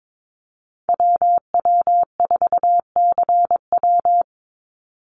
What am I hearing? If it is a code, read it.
Morse code, heard as a keyed tone, about 22 wpm, 700 Hz. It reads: WW4CW
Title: USA Call Signs 22wpm